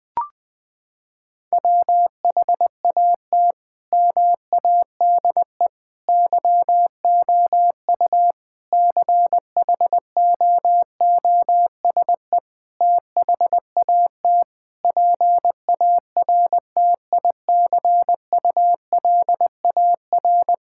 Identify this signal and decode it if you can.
Morse code, heard as a keyed tone, about 20 wpm, 700 Hz. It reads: WHAT MADE YOU CHOOSE THAT PARTICULAR
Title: Sentences from Top 1000 Words - Review Part 5 - 20wpm